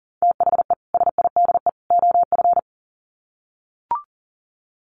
Morse code, heard as a keyed tone, about 40 wpm, 700 Hz. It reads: THE SIDE OF
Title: Sets of 3 Words 40wpm